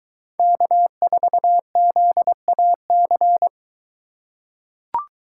Morse code, heard as a keyed tone, about 21 wpm, 700 Hz. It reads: K4ZAC